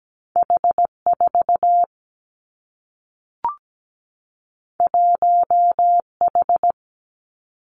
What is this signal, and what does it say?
Morse code, heard as a keyed tone, about 17 wpm, 700 Hz. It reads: H 4 1 H
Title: Two Challenging Letter-Number Combinations 17wpm